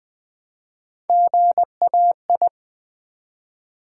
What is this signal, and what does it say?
Morse code, heard as a keyed tone, about 20 wpm, 700 Hz. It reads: GAI
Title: Build A Word - 1 Letter At A Time - 3 to 5-Letter Words - 20wpm